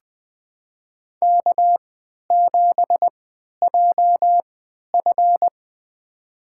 Morse code, heard as a keyed tone, about 20 wpm, spaced as for 11 wpm, 700 Hz. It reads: K7JF
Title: USA Call Signs - 4 Characters - 3x Character Spacing - 20wpm